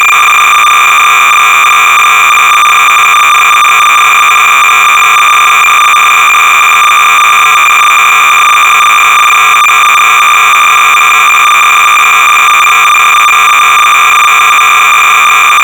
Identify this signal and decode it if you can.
Cassette data (Kansas City type).
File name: sorcerer / 300baud / Z-80 Dev Pac Extension Load Module-(1980)(Don Ursem)(Quality Software)-Sorcerer-300baud